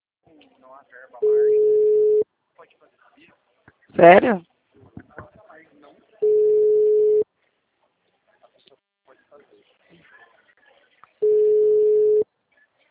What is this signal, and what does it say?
Ringback tone (Europe, 425 Hz), with speech.
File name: out- -2019- - - .0